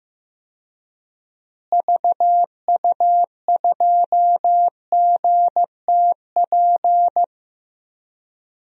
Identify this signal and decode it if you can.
Morse code, heard as a keyed tone, about 16 wpm, 700 Hz. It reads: VU2GTP
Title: DXCC-100 Call Signs - 6 Characters - Repeated 3x - 15wpm